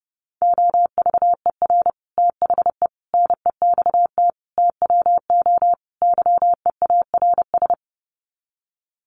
Morse code, heard as a keyed tone, about 29 wpm, 700 Hz. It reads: OVER THE NEXT TWO YEARS
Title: Sets of 5 Words Encore2 30wpm